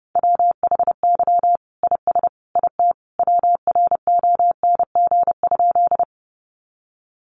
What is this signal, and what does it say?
Morse code, heard as a keyed tone, about 28 wpm, 700 Hz. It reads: WHY IS IT WRONG?
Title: Sentences from Top 900 Words 30wpm